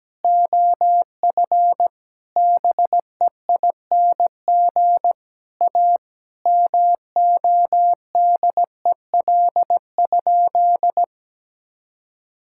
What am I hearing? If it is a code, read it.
Morse code, heard as a keyed tone, about 17 wpm, 700 Hz. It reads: OF BEING A MODEL?
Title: Sentences from Top 1000 Words - Review Part 5 - 17wpm